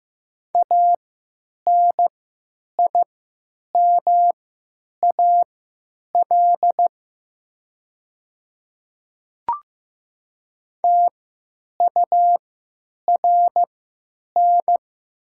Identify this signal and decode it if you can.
Morse code, heard as a keyed tone, about 15 wpm, spaced as for 9 wpm, 700 Hz. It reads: ANIMAL TURN
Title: 3x Character Spacing - Top 500 Words - 15wpm